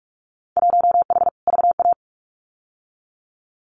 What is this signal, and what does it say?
Morse code, heard as a keyed tone, about 44 wpm, 700 Hz. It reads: 1H VA
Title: Winter Field Day Exchanges 45wpm